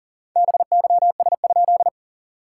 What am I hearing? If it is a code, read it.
Morse code, heard as a keyed tone, about 39 wpm, 690 Hz. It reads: BYS?
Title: SOTA QSO Practice 40wpm